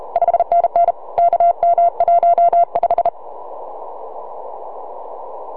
Morse code, heard as a keyed tone, about 34 wpm, 690 Hz. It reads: 5NN KM15
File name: sy9vhf 6m